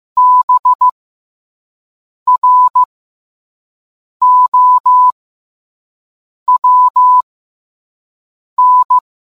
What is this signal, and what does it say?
Morse code, heard as a keyed tone, about 15 wpm, spaced as for 5 wpm, 1000 Hz. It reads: BROWN